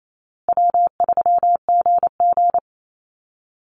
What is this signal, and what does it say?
Morse code, heard as a keyed tone, about 28 wpm, 700 Hz. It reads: W3GG
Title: Easy Call Signs Repeated 3x - 28wpm